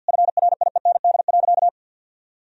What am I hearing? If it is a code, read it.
Morse code, heard as a keyed tone, about 50 wpm, 700 Hz. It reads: FRIEND.
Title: Sentences from Top 1000 Words - Review Part 3 - 50wpm